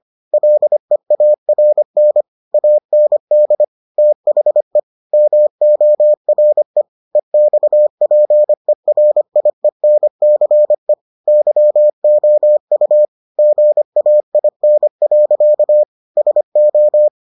Morse code, heard as a keyed tone, about 25 wpm, 600 Hz. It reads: LEARN AND THE MORE EXPERIENCE YOU GAIN. SO